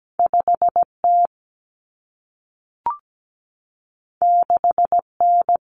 Morse code, heard as a keyed tone, about 17 wpm, 700 Hz. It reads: 5T 6N
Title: Common Cut Numbers - 2-Digit 17wpm